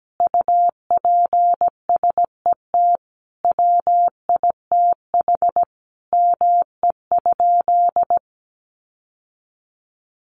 Morse code, heard as a keyed tone, about 17 wpm, 700 Hz. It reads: UPSET WITH ME?
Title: Sentences from Top 800 Words 17wpm